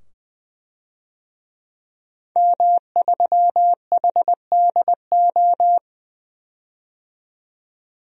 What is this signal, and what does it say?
Morse code, heard as a keyed tone, about 20 wpm, 700 Hz. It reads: M3HDO
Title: DXCC-100 Call Signs - 5 Characters - 20wpm